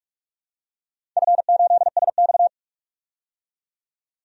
Morse code, heard as a keyed tone, about 46 wpm, 700 Hz. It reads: F8SX
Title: DXCC-100 Call Signs - 4 Characters - 45wpm